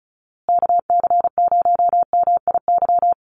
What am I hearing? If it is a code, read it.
Morse code, heard as a keyed tone, about 34 wpm, 700 Hz. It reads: KC0MIY
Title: USA Call Signs Repeated 3x 35wpm